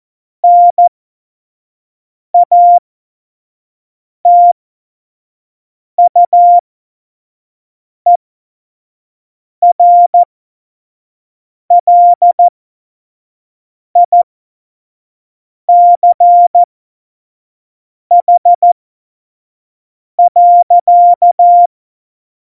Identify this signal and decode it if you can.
Morse code, heard as a keyed tone, about 14 wpm, spaced as for 5 wpm, 700 Hz. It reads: NATUERLICH.
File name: Morsekurs / Witze / Witz 12 Mai 2020, 25 F70